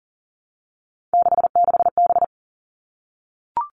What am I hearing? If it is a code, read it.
Morse code, heard as a keyed tone, about 40 wpm, 700 Hz. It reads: 66B
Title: All Cut Numbers - 3-Digit 40wpm